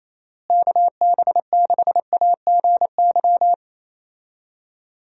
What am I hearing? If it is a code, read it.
Morse code, heard as a keyed tone, about 27 wpm, 700 Hz. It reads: KB6AGY
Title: USA Call Signs - 6 Characters - Repeated 3x - 28wpm